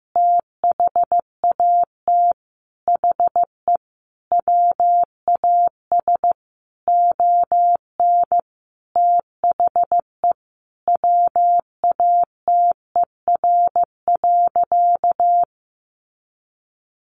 Morse code, heard as a keyed tone, about 15 wpm, 700 Hz. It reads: THAT HE WAS ON THE WATER.
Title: Sentences from Top 500 Words 15wpm